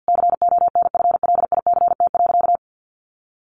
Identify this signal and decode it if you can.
Morse code, heard as a keyed tone, about 50 wpm, 700 Hz. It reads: CONFLICT.
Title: Sentences from Top 1300 Words 50wpm